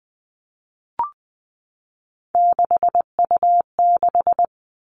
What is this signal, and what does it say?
Morse code, heard as a keyed tone, about 20 wpm, 700 Hz. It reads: 6U6
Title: All Cut Numbers - 3-Digit 20wpm